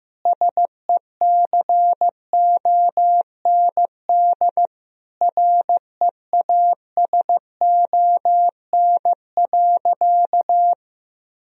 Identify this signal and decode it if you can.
Morse code, heard as a keyed tone, about 15 wpm, 700 Hz. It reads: SECOND REASON.
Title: Sentences from Top 1000 Words - Review Part 1 - 15wpm